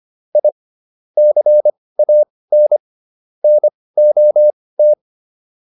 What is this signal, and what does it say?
Morse code, heard as a keyed tone, about 25 wpm, spaced as for 18 wpm, 600 Hz. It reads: I CAN NOT